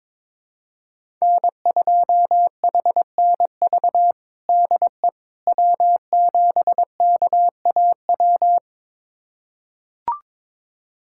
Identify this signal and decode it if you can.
Morse code, heard as a keyed tone, about 22 wpm, 700 Hz. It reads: N2HNV DE W7KAW
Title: USA Callsign DE Callsign - 5 Characters - 22wpm